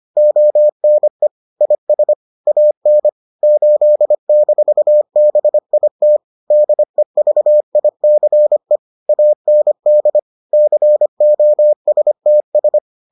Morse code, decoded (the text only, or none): ONE IS AN 8-BIT DEVICE AND COSTS